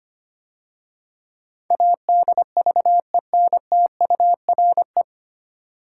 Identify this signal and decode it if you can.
Morse code, heard as a keyed tone, about 25 wpm, 690 Hz. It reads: ADVENTURE
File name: practice / 25wpm / 9 Letter Words - First 3 Letters Given - 25wpm